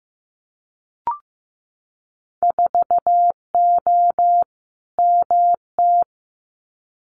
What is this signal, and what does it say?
Morse code, heard as a keyed tone, about 15 wpm, 700 Hz. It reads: 4O MT
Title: Winter Field Day Exchanges Encore 15wpm